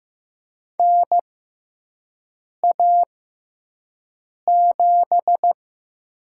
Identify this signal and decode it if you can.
Morse code, heard as a keyed tone, about 15 wpm, spaced as for 5 wpm, 700 Hz. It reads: NA7
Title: USA Call Signs - 4 Characters - 6x Character Spacing - 15wpm